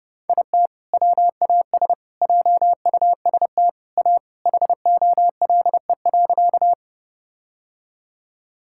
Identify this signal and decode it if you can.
Morse code, heard as a keyed tone, about 32 wpm, 700 Hz. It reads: IT WAS JUST A HOLE.
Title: Sentences from Top 1600 Words 30wpm